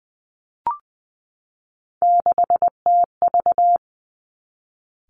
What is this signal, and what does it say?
Morse code, heard as a keyed tone, about 19 wpm, 700 Hz. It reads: 6TV
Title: Three Challenging Letter-Number Combinations 20wpm